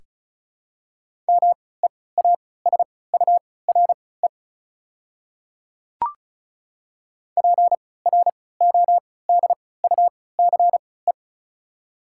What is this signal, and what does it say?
Morse code, heard as a keyed tone, about 36 wpm, spaced as for 20 wpm, 700 Hz. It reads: MEASURE PRODUCE
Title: 3x Character Spacing - Top 500 Words - 35wpm